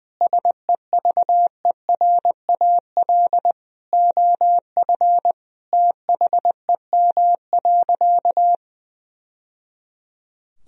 Morse code, heard as a keyed tone, about 20 wpm, 700 Hz. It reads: SEVERAL OF THEM.
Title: Sentences from Top 1000 Words - Review Part 2 - 20wpm